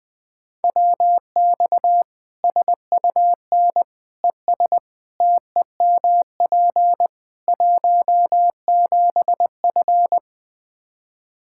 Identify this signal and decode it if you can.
Morse code, heard as a keyed tone, about 20 wpm, 700 Hz. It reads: WX SUN ES TEMP 17F
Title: Practice QSO - Exchange 2 - Info - 20wpm